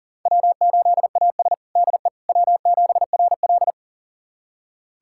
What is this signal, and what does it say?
Morse code, heard as a keyed tone, about 40 wpm, 700 Hz. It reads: W8AS DE W7RL
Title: USA Callsign DE Callsign - 4 Characters - 40wpm